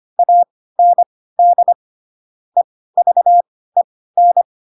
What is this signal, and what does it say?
Morse code, heard as a keyed tone, about 25 wpm, spaced as for 16 wpm, 700 Hz. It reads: AND EVEN
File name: Pinocchio / Pinocchio ch5 10 25 ik1hge